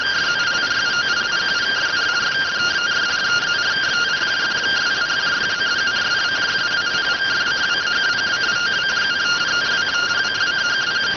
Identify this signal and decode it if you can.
Two-tone teleprinter signal, about 170 Hz shift.